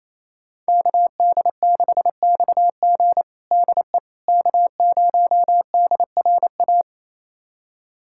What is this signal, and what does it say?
Morse code, heard as a keyed tone, about 28 wpm, 700 Hz. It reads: KD6XG DE K0DRA
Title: USA Callsign DE Callsign - 5 Characters - 28wpm